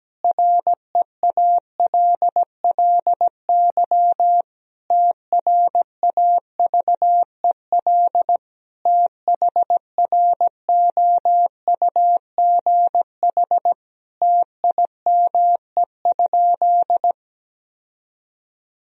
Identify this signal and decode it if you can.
Morse code, heard as a keyed tone, about 17 wpm, 700 Hz. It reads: REALLY TRAVEL THROUGH TIME?
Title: Sentences from Top 1000 Words - Review Part 2 - 17wpm